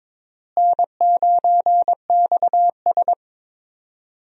Morse code, heard as a keyed tone, about 22 wpm, 700 Hz. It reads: N9XS
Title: USA Call Signs - 4 Characters - Repeated 3x - 22wpm